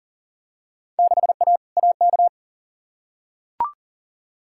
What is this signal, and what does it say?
Morse code, heard as a keyed tone, about 40 wpm, 700 Hz. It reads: 6A AK
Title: Field Day Exchanges Encore - 40wpm